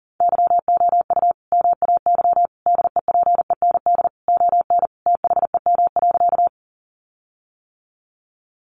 Morse code, heard as a keyed tone, about 40 wpm, 700 Hz. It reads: YOU MAY DEPEND ON THEM.
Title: Sentences from Top 1300 Words 40wpm